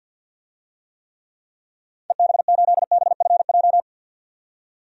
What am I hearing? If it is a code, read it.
Morse code, heard as a keyed tone, about 50 wpm, 700 Hz. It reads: EB8BFJ
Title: DXCC-100 Call Signs - 6 Characters - 50wpm